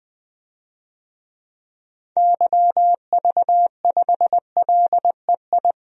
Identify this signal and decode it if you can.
Morse code, heard as a keyed tone, about 20 wpm, 700 Hz. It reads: YV5LEI
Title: DXCC-100 Call Signs - 6 Characters - 20wpm